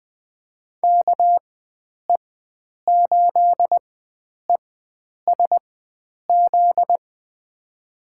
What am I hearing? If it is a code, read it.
Morse code, heard as a keyed tone, about 20 wpm, spaced as for 9 wpm, 700 Hz. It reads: KE8ESZ
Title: USA Call Signs - 6 Characters - 4x Character Spacing - 20wpm